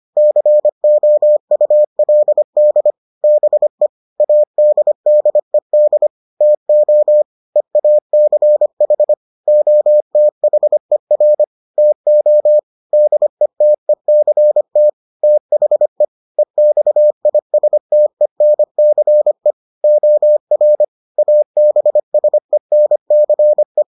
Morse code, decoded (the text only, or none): COULD BE ADDED TO EACH OTHER TO DETECT THE EXISTENCE OR ABSENCE